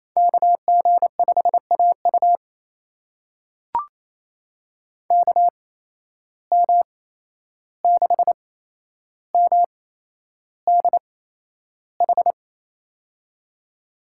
Morse code, heard as a keyed tone, about 28 wpm, spaced as for 11 wpm, 700 Hz. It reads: KG5AU KM6MDH
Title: USA Call Signs - 8x Character Spacing - 28wpm